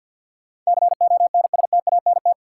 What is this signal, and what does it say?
Morse code, heard as a keyed tone, about 47 wpm, 700 Hz. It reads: CONSTANT
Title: Sentences from Top 1500 Words - New Word Drill Alt - 50wpm